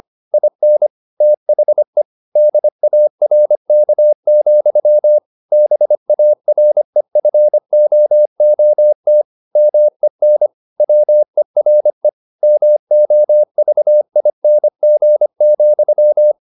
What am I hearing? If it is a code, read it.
Morse code, heard as a keyed tone, about 25 wpm, 600 Hz. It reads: IN THE DARK, BAREFOOT MEN WERE MOVING,
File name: the old man and the sea part 1 25wpm